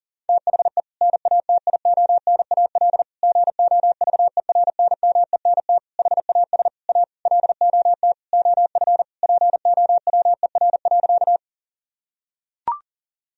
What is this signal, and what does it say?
Morse code, heard as a keyed tone, about 40 wpm, 700 Hz. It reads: THE NATIONAL GOVERNMENT HAS A LOT OF POWER.
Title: Sentences from Top 1000 Words - Review Part 4 - 40wpm